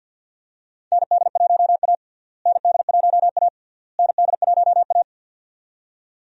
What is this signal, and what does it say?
Morse code, heard as a keyed tone, about 51 wpm, 700 Hz. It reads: ND1A ND1A ND1A
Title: USA Call Signs - 4 Characters - Repeated 3x - 50wpm